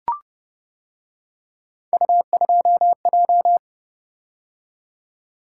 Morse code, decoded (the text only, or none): U2J